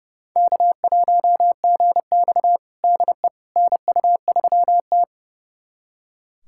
Morse code, heard as a keyed tone, about 30 wpm, 700 Hz. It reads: K1GX DE NU3T
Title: USA Callsign DE Callsign - 4 Characters - 30wpm